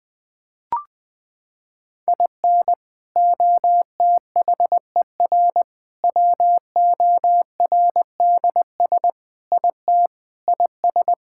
Morse code, heard as a keyed tone, about 21 wpm, 700 Hz. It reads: IN OTHER WORDS IT IS
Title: Sentences with 3-Word Phrases - 20wpm